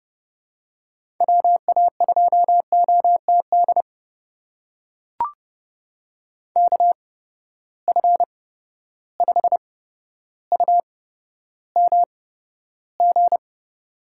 Morse code, heard as a keyed tone, about 30 wpm, spaced as for 13 wpm, 700 Hz. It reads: WA2OTD KF5UMG
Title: USA Call Signs - 6 Characters - 8x Character Spacing - 30wpm